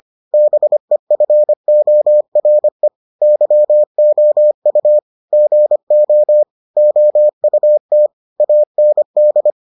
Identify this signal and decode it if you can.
Morse code, heard as a keyed tone, about 25 wpm, 600 Hz. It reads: BEFORE YOU GO OUT AND